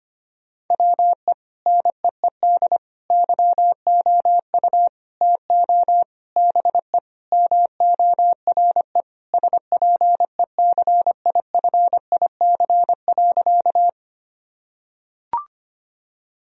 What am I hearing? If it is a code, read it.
Morse code, heard as a keyed tone, about 25 wpm, 700 Hz. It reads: WE NEED YOU TO BE MORE SPECIFIC.
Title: Sentences from Top 1000 Words - Review Part 3 - 25wpm